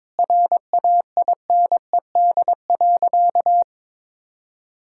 Morse code, heard as a keyed tone, about 22 wpm, 700 Hz. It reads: RAINED.
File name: practice / 22wpm / Sentences with 3-Word Phrases - 22wpm